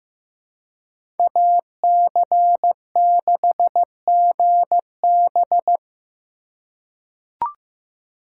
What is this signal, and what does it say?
Morse code, heard as a keyed tone, about 15 wpm, 700 Hz. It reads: AC6GB
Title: USA Call Signs - 5 Characters - 4x Character Spacing - 15wpm